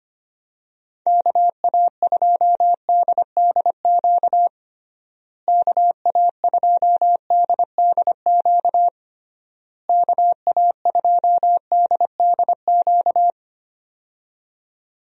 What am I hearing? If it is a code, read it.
Morse code, heard as a keyed tone, about 25 wpm, 700 Hz. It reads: KA2DDQ KA2DDQ KA2DDQ
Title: USA Call Signs - 6 Characters - Repeated 3x - 25wpm